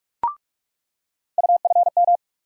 Morse code, heard as a keyed tone, about 46 wpm, 700 Hz. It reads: UFM
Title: USA Call Sign Postfixes - 3 Letters 45wpm